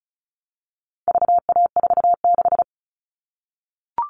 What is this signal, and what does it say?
Morse code, heard as a keyed tone, about 34 wpm, 700 Hz. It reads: VA46
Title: All Cut Numbers - 4-Digit 35wpm